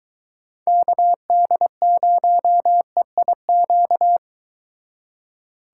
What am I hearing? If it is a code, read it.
Morse code, heard as a keyed tone, about 22 wpm, 700 Hz. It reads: KD0EIQ